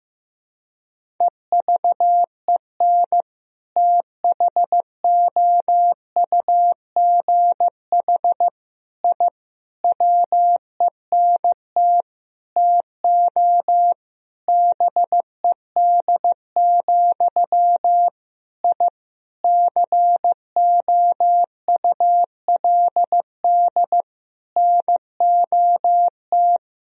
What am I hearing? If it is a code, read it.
Morse code, heard as a keyed tone, about 15 wpm, 700 Hz. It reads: EVEN THOUGH I WENT TO BED, I COULD NOT
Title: Sentences from Top 1000 Words - Review Part 2 - 15wpm